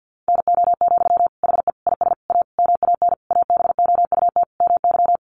Morse code, heard as a keyed tone, about 50 wpm, 700 Hz. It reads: NO, HE IS A MAN ABOUT MY
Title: Sentences from Top 400 Words 50wpm